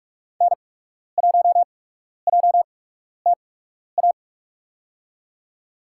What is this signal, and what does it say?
Morse code, heard as a keyed tone, about 45 wpm, spaced as for 12 wpm, 700 Hz. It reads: N1JTA
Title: USA Call Signs - 5 Characters - 8x Character Spacing - 45wpm